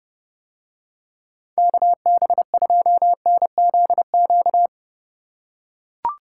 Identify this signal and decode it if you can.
Morse code, heard as a keyed tone, about 30 wpm, 700 Hz. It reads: KB2NZQ